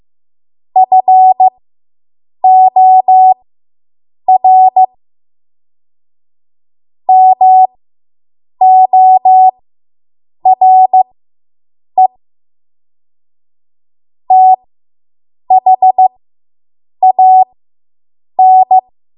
Morse code, heard as a keyed tone, about 15 wpm, spaced as for 7 wpm, 750 Hz. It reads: FOR MORE THAN